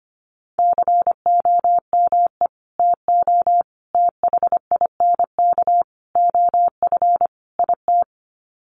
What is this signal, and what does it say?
Morse code, heard as a keyed tone, about 25 wpm, 700 Hz. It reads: COME TO THINK OF IT
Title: Sets of 5 Words Encore 25wpm